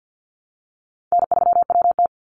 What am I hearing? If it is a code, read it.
Morse code, heard as a keyed tone, about 50 wpm, 700 Hz. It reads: N3WT
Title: Easy Call Signs Repeated 2x - 50wpm